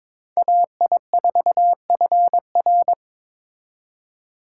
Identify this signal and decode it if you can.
Morse code, heard as a keyed tone, about 26 wpm, 700 Hz. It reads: AI4FR
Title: Speed Racing - USA Call Signs 22wpm